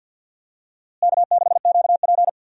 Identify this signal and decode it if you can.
Morse code, heard as a keyed tone, about 48 wpm, 700 Hz. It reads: K6QP
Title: USA Call Signs - 4 Characters - Repeated 2x - 50wpm